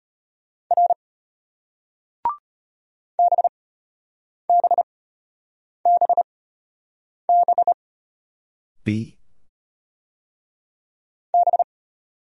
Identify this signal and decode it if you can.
Morse code, heard as a keyed tone, about 33 wpm, spaced as for 8 wpm, 700 Hz. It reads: R BBBB B